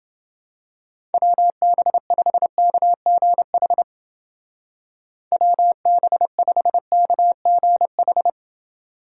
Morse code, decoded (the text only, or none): WB5KGH WB5KGH